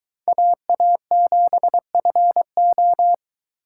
Morse code, heard as a keyed tone, about 23 wpm, 700 Hz. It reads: AA7FO